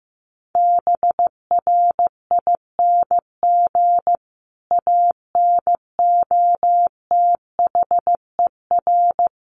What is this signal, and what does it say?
Morse code, heard as a keyed tone, about 15 wpm, 700 Hz. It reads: BRING ANOTHER
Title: Sentences from Top 1300 Words 15wpm